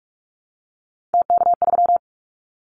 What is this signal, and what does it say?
Morse code, heard as a keyed tone, about 46 wpm, 700 Hz. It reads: TK3
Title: Callsign Prefixes for Top 100 DXCC - 45wpm